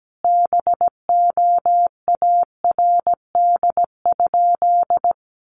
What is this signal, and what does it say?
Morse code, heard as a keyed tone, about 17 wpm, 700 Hz. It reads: BOARD?
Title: Sentences from Top 1300 Words 17wpm